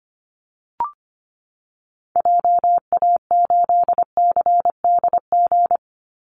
Morse code, heard as a keyed tone, about 25 wpm, 700 Hz. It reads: JA8CDG
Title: Call Signs Repeated 2x - 25wpm